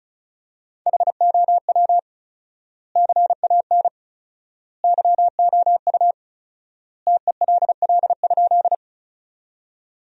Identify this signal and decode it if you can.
Morse code, heard as a keyed tone, about 36 wpm, 700 Hz. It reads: HOW CAN YOU TELL?
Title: 4x Word-Spacing - Sentences from Top 200 Words - 35wpm